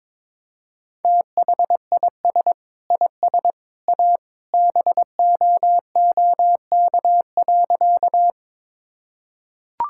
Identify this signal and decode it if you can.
Morse code, heard as a keyed tone, about 22 wpm, 700 Hz. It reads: THIS IS A BOOK.
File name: practice / 22wpm / Sentences with 3-Word Phrases - 22wpm